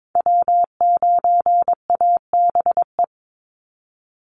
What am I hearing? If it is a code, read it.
Morse code, heard as a keyed tone, about 22 wpm, 700 Hz. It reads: W9ABE
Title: USA Call Signs 22wpm